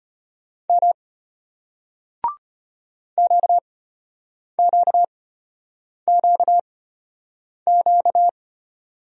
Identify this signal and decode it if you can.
Morse code, heard as a keyed tone, about 34 wpm, spaced as for 8 wpm, 700 Hz. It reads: M QQQQ